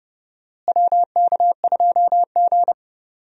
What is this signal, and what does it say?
Morse code, heard as a keyed tone, about 29 wpm, 700 Hz. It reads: WK2G
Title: USA Call Signs - 4 Characters - Repeated 3x - 30wpm